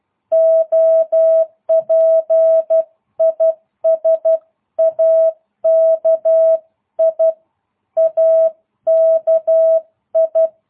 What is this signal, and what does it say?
Morse code, heard as a keyed tone, about 12 wpm, 660 Hz. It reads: OP IS AKI AKI